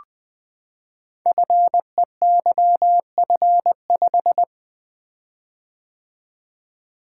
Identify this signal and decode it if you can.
Morse code, heard as a keyed tone, about 20 wpm, 700 Hz. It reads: FEYF5
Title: Code Groups - 5 Letter:Numbers - 20wpm